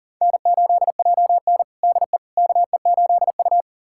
Morse code, heard as a keyed tone, about 40 wpm, 700 Hz. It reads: N8JN DE KE8U
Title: USA Callsign DE Callsign - 4 Characters - 40wpm